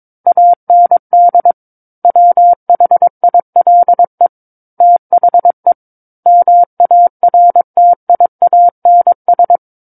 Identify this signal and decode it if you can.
Morse code, heard as a keyed tone, about 22 wpm, 700 Hz. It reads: AND WHILE THE MARTIANS
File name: Book One / 13-Chapter 13 - 22 WPM